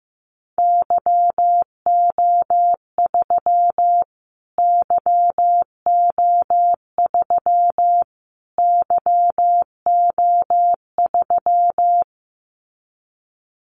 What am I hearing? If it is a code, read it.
Morse code, heard as a keyed tone, about 15 wpm, 700 Hz. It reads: YO3 YO3 YO3
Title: Callsign Prefixes for Top 25 DXCC - Repeated 3x - 15wpm